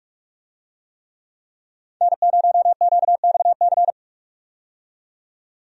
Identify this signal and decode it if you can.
Morse code, heard as a keyed tone, about 46 wpm, 700 Hz. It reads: N0QXC
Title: USA Call Signs - 5 Characters - 45wpm